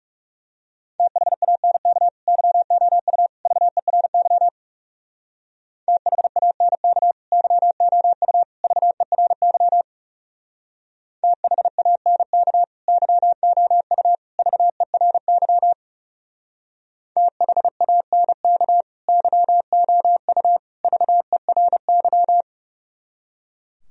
Morse code, heard as a keyed tone, about 36 wpm, 700 Hz. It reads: THANK YOU VERY THANK YOU VERY THANK YOU VERY THANK YOU VERY
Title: Speed-Racing - Sets of 3 Words 30wpm